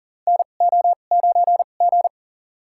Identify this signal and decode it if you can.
Morse code, heard as a keyed tone, about 40 wpm, spaced as for 29 wpm, 700 Hz. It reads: NO9G
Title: USA Call Signs - 4 Characters - 2x Character Spacing - 40wpm